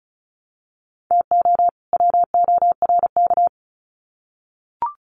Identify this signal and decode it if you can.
Morse code, heard as a keyed tone, about 34 wpm, 700 Hz. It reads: TO WORK
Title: Sets of 2 Words 35wpm